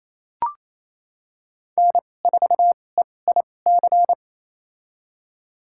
Morse code, heard as a keyed tone, about 28 wpm, spaced as for 20 wpm, 700 Hz. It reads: N4EIC
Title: USA Call Signs - 5 Characters - 2x Character Spacing - 28wpm